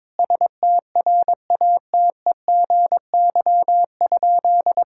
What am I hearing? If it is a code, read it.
Morse code, heard as a keyed tone, about 22 wpm, 700 Hz. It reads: STRATEGY?
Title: Sentences from Top 1000 Words - Review Part 6 - 22wpm